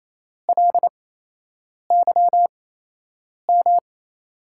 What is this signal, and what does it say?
Morse code, heard as a keyed tone, about 28 wpm, spaced as for 8 wpm, 700 Hz. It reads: LYM
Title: USA Call Signs - 6 Characters - 8x Character Spacing - 28wpm